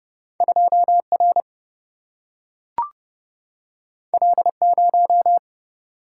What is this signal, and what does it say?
Morse code, heard as a keyed tone, about 30 wpm, 700 Hz. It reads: 2 R L 0
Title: Single-Letter Single-Number 30wpm